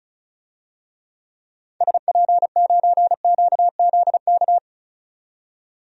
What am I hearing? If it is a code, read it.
Morse code, heard as a keyed tone, about 36 wpm, 700 Hz. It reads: SP9QZK
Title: DXCC-100 Call Signs - 6 Characters - 35wpm